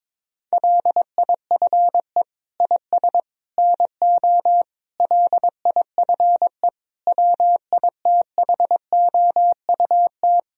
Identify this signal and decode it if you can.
Morse code, heard as a keyed tone, about 22 wpm, 700 Hz. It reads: LIFE IS NO LIFE WITHOUT
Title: Sentences from Top 1000 Words - Review Part 3 - 22wpm